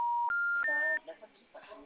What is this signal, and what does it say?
Special information tone, with speech.